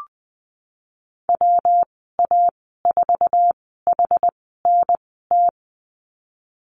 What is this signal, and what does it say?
Morse code, heard as a keyed tone, about 20 wpm, spaced as for 15 wpm, 700 Hz. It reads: WA4HNT
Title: 128 Lesson 21g - Number 6 - Call Signs 20wpm